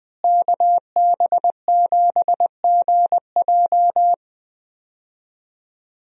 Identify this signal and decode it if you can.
Morse code, heard as a keyed tone, about 20 wpm, 700 Hz. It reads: KB7GJ
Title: USA Call Signs - 5 Characters - Repeated 3x - 20wpm